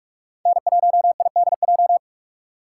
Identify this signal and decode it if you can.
Morse code, heard as a keyed tone, about 46 wpm, 700 Hz. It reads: N1IDJ
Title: USA Call Signs - 5 Characters - 3x Character Spacing - 45wpm